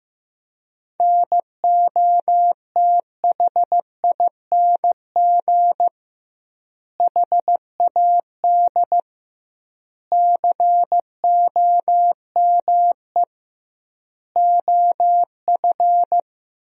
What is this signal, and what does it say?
Morse code, heard as a keyed tone, about 15 wpm, 700 Hz. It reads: NOTHING HAD COME OF
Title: 2x Word-Spacing - Sentences from Top 500 Words - 15wpm